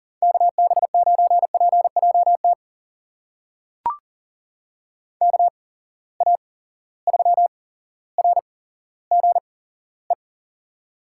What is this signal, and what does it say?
Morse code, heard as a keyed tone, about 40 wpm, spaced as for 17 wpm, 700 Hz. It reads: KB9PJT KA3RGE
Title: USA Call Signs - 6 Characters - 8x Character Spacing - 40wpm